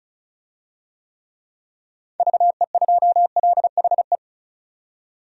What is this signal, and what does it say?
Morse code, heard as a keyed tone, about 36 wpm, 700 Hz. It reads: VE2LHE
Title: DXCC-100 Call Signs - 6 Characters - 35wpm